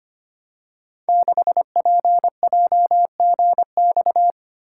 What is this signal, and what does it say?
Morse code, heard as a keyed tone, about 25 wpm, 700 Hz. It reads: 6PJGX
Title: Code Groups - 5 Letter:Numbers - 25wpm